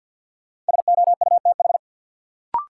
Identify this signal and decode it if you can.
Morse code, heard as a keyed tone, about 47 wpm, 700 Hz. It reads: SOUTH
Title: Sentences from Top 400 Words - New Word Drill - 50wpm